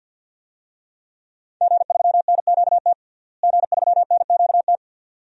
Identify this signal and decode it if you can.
Morse code, heard as a keyed tone, about 50 wpm, 700 Hz. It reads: G3NQT G3NQT
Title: DXCC-100 Call Signs - 5 Characters - Repeated 2x - 50wpm